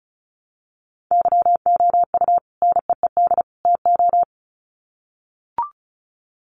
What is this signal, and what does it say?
Morse code, heard as a keyed tone, about 34 wpm, 700 Hz. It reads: YOU NEED TO
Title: Sets of 3 Words 35wpm